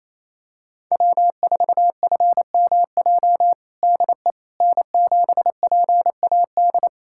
Morse code, heard as a keyed tone, about 28 wpm, 700 Hz. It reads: W4FMJ DE N7PAD
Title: USA Callsign DE Callsign - 5 Characters - 28wpm